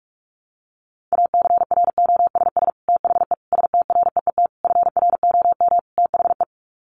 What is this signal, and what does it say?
Morse code, heard as a keyed tone, about 44 wpm, 700 Hz. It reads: ACROSS THE STREET FROM THE
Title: Sets of 5 Words Encore 45wpm